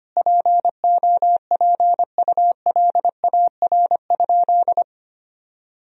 Morse code, heard as a keyed tone, about 25 wpm, 700 Hz. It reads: POPULAR?
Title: Sentences from Top 1000 Words - Review Part 3 - 25wpm